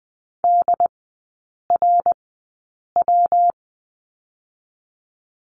Morse code, heard as a keyed tone, about 20 wpm, spaced as for 8 wpm, 700 Hz. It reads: DRW
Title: 114 Lesson 19e - Letter w - Three Characters 20wpm